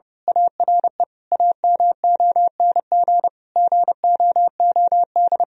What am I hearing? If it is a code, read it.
Morse code, heard as a keyed tone, about 30 wpm, 700 Hz. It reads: ARE AMONG GOOD
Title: Sentences from Top 1000 Words - Review Part 3 - 30wpm